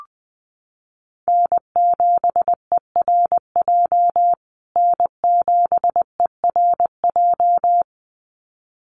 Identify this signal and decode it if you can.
Morse code, heard as a keyed tone, about 20 wpm, 700 Hz. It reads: N7ERJ N7ERJ
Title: USA Call Signs Repeated 2x 20wpm